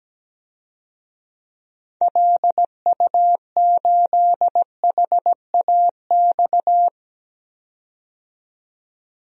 Morse code, heard as a keyed tone, about 17 wpm, 700 Hz. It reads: LU8HAX
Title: DXCC-100 Call Signs - 6 Characters - 17wpm